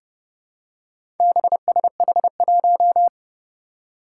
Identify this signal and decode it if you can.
Morse code, heard as a keyed tone, about 30 wpm, 700 Hz. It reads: BSH1
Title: Four Challenging Letter-Number Combinations 30wpm